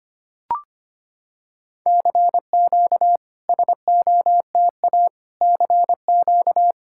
Morse code, heard as a keyed tone, about 25 wpm, 700 Hz. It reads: CQ SOTA CQ
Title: SOTA QSO Practice 25wpm